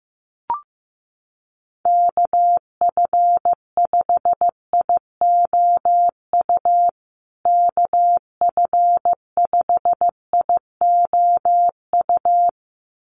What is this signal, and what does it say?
Morse code, heard as a keyed tone, about 15 wpm, 700 Hz. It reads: KF5IOU KF5IOU
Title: USA Call Signs Repeated 2x 15wpm